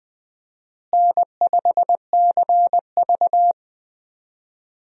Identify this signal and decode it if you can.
Morse code, heard as a keyed tone, about 20 wpm, 700 Hz. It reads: N5CV
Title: USA Call Signs - 4 Characters - Repeated 3x - 20wpm